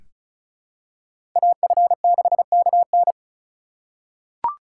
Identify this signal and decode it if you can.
Morse code, heard as a keyed tone, about 36 wpm, 700 Hz. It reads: AF6KN
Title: USA Call Signs - 5 Characters - 2x Character Spacing - 35wpm